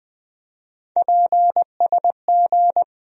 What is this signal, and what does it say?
Morse code, heard as a keyed tone, about 20 wpm, 700 Hz. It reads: PSG
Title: USA Call Sign Postfixes - 3 Letters 20wpm